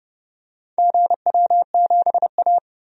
Morse code, heard as a keyed tone, about 30 wpm, 700 Hz. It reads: GW7A
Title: DXCC-100 Call Signs - 4 Characters - Repeated 2x - 30wpm